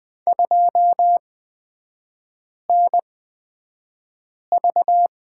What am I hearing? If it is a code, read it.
Morse code, heard as a keyed tone, about 20 wpm, spaced as for 5 wpm, 700 Hz. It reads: 2NV
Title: 236 Lesson 38e - Letter q - Three Characters 20wpm